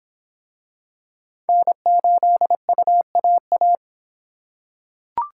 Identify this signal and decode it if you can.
Morse code, heard as a keyed tone, about 23 wpm, 700 Hz. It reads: N8UAA